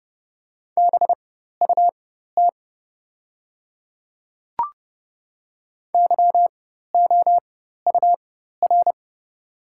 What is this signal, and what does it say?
Morse code, heard as a keyed tone, about 30 wpm, spaced as for 14 wpm, 700 Hz. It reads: BUT YOUR
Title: 4x Character Spacing - Top 500 Words - 30wpm